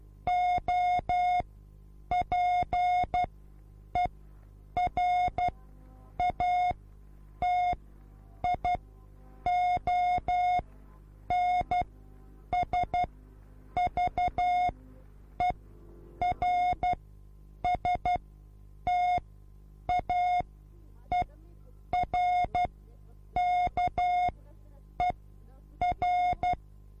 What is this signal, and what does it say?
Morse code, heard as a keyed tone, about 12 wpm, spaced as for 8 wpm, 720 Hz. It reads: OPERATIONSVERSTAERKER
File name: repos / Morsekurs / DJ4UF Morsen24